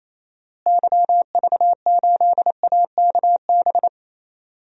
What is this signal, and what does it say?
Morse code, heard as a keyed tone, about 28 wpm, 700 Hz. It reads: YV8AKB
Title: DXCC-100 Call Signs - 6 Characters - Repeated 3x - 28wpm